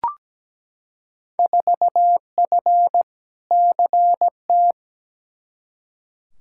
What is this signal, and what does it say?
Morse code, heard as a keyed tone, about 17 wpm, 700 Hz. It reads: 4F CT
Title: Field Day Exchanges Encore - 17wpm